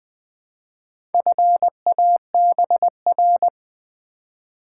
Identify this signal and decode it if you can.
Morse code, heard as a keyed tone, about 20 wpm, 700 Hz. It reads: FABR
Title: Build A Word - 2 Letters At A Time - 6-Letter Words - 20wpm